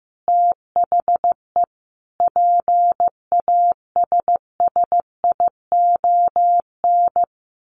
Morse code, heard as a keyed tone, about 15 wpm, 700 Hz. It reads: THE PASSION
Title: Sentences from Top 1400 Words 15wpm